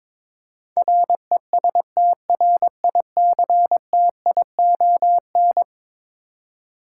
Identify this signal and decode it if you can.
Morse code, heard as a keyed tone, about 22 wpm, 700 Hz. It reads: RESTRICTION
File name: practice / 22wpm / Eleven Letter Words - 22wpm